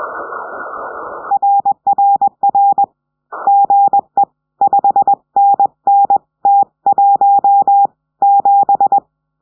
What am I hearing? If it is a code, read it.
Morse code, heard as a keyed tone, about 21 wpm, 800 Hz. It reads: RRR GE 5NN T17
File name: 70cm